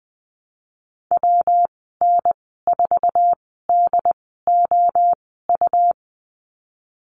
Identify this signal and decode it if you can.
Morse code, heard as a keyed tone, about 20 wpm, spaced as for 15 wpm, 700 Hz. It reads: WN4DOU
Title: 128 Lesson 21g - Number 6 - Call Signs 20wpm